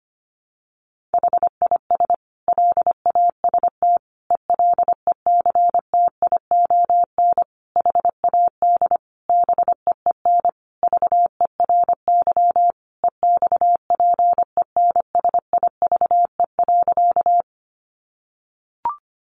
Morse code, heard as a keyed tone, about 25 wpm, 700 Hz. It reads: HIS LAST ELECTION HAD BEEN VERY EXPENSIVE.
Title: Sentences from Top 1500 Words 25wpm